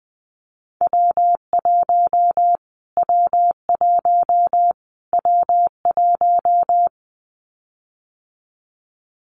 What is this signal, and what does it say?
Morse code, heard as a keyed tone, about 20 wpm, 700 Hz. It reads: W1 W1 W1
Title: Callsign Prefixes for Top 25 DXCC - Repeated 3x - 20wpm